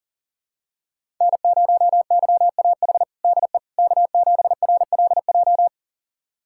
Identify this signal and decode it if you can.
Morse code, heard as a keyed tone, about 40 wpm, 700 Hz. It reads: N0YAH DE K7RLJ
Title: USA Callsign DE Callsign - 5 Characters - 40wpm